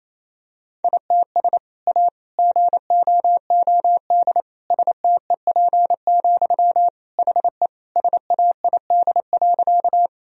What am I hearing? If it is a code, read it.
Morse code, heard as a keyed tone, about 27 wpm, 700 Hz. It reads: ITS A GOOD STEP, HE SAID.
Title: Sentences from Top 1000 Words - Review Part 2 - 28wpm